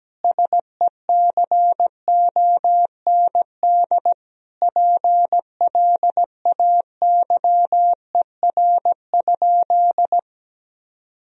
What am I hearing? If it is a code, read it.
Morse code, heard as a keyed tone, about 17 wpm, 700 Hz. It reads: SECOND PLAYER?
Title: Sentences from Top 1000 Words - Review Part 4 - 17wpm